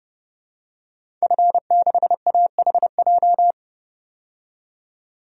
Morse code, heard as a keyed tone, about 32 wpm, 700 Hz. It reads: F6AHJ
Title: DXCC-100 Call Signs - 5 Characters - 30wpm